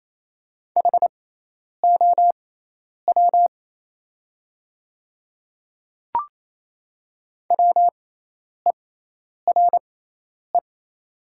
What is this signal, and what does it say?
Morse code, heard as a keyed tone, about 28 wpm, spaced as for 10 wpm, 700 Hz. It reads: HOW WERE
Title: 6x Character Spacing - Top 100 Words - 28wpm